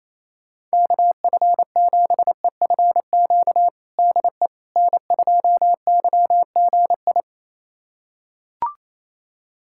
Morse code, heard as a keyed tone, about 28 wpm, 700 Hz. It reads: KF7EFQ DE N2YGI
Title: USA Callsign DE Callsign 28wpm